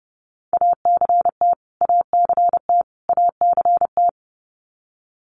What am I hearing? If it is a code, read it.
Morse code, heard as a keyed tone, about 30 wpm, 700 Hz. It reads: ACT ACT ACT
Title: Top 500 Words Repeated 30wpm